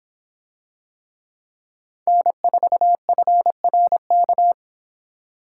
Morse code, heard as a keyed tone, about 26 wpm, 700 Hz. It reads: N4FRK